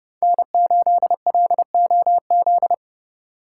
Morse code, heard as a keyed tone, about 32 wpm, 700 Hz. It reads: N8LOZ